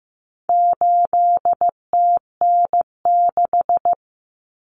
Morse code, heard as a keyed tone, about 15 wpm, 700 Hz. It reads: 8TN6
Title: Common Cut Numbers - 4-Digit 15wpm